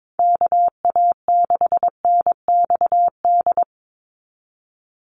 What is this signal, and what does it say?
Morse code, heard as a keyed tone, about 22 wpm, 700 Hz. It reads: KA6NXD
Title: USA Call Signs 22wpm